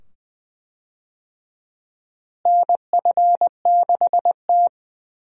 Morse code, heard as a keyed tone, about 20 wpm, 700 Hz. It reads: NF6T
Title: USA Call Signs - 4 Characters - Repeated 2x - 20wpm